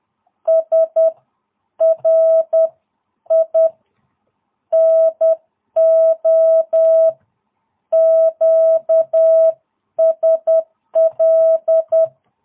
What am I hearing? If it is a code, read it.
Morse code, heard as a keyed tone, about 10 wpm, spaced as for 9 wpm, 660 Hz. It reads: SRI NO QSL
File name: jh8cbh / 2022cw